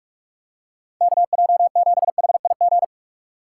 Morse code, heard as a keyed tone, about 46 wpm, 700 Hz. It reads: KJ7HIG
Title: USA Call Signs - 3x Character Spacing - 45wpm